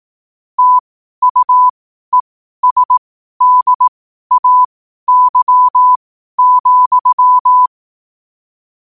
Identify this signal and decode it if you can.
Morse code, heard as a keyed tone, about 18 wpm, spaced as for 12 wpm, 1000 Hz. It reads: TUESDAY,